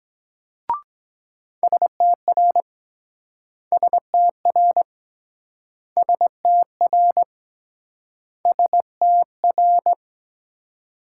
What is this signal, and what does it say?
Morse code, heard as a keyed tone, about 23 wpm, 690 Hz. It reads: STR STR STR STR